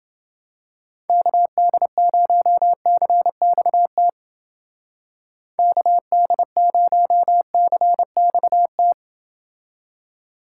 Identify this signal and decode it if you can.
Morse code, heard as a keyed tone, about 29 wpm, 700 Hz. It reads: KD0CXT KD0CXT